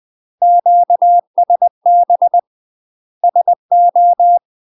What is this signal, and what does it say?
Morse code, heard as a keyed tone, about 20 wpm, 700 Hz. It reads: QSB SO
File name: UK MASTER QSOs TRAINING / 20WPM / UK MASTER QSOs 0001